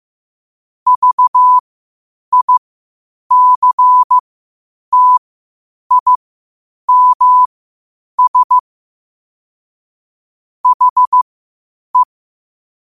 Morse code, heard as a keyed tone, about 15 wpm, spaced as for 8 wpm, 1000 Hz. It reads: VICTIMS HE
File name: qotdfiles / 8wpm